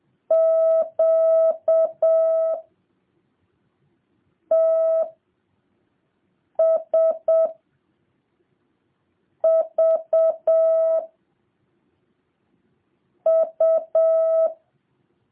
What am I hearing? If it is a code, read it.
Morse code, heard as a keyed tone, about 7 wpm, 650 Hz. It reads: Q T S V U